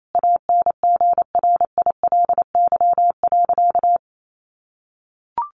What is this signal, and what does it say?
Morse code, heard as a keyed tone, about 28 wpm, 700 Hz. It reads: ANGRILY.
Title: A Sound Of Thunder 28wpm